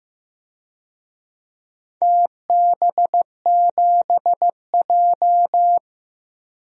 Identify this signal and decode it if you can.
Morse code, heard as a keyed tone, about 15 wpm, 700 Hz. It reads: TB7J
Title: DXCC-100 Call Signs - 4 Characters - Repeated 2x - 15wpm